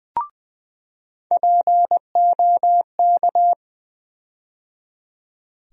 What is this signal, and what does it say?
Morse code, heard as a keyed tone, about 20 wpm, 700 Hz. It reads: POK
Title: USA Call Sign Postfixes - 3 Letters 20wpm